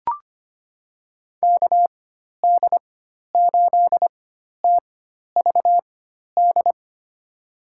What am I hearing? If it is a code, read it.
Morse code, heard as a keyed tone, about 25 wpm, spaced as for 12 wpm, 700 Hz. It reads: KD8TVD
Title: USA Call Signs - 6 Characters - 4x Character Spacing - 25wpm